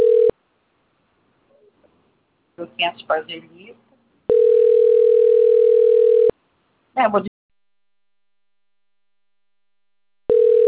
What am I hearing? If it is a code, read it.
Ringback tone (North America), 440 + 480 Hz, with speech.
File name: rg-1600-unknown- - - .19040